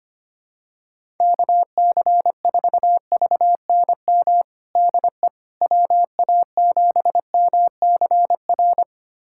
Morse code, heard as a keyed tone, about 25 wpm, 700 Hz. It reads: KC4VNM DE WA7MCR
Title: USA Callsign DE Callsign - 6 Characters - 25wpm